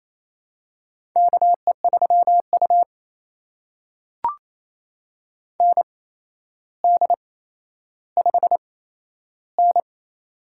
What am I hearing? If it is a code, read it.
Morse code, heard as a keyed tone, about 28 wpm, spaced as for 12 wpm, 700 Hz. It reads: KE3U ND5N
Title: USA Call Signs - 4 Characters - 8x Character Spacing - 28wpm